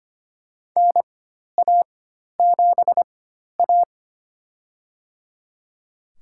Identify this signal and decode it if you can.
Morse code, heard as a keyed tone, about 25 wpm, spaced as for 12 wpm, 700 Hz. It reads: NA7A
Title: USA Call Signs - 4 Characters - 4x Character Spacing - 25wpm